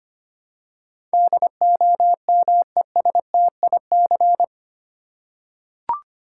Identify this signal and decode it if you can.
Morse code, heard as a keyed tone, about 25 wpm, 700 Hz. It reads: DOMESTIC
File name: Eight Letter Words - 25wpm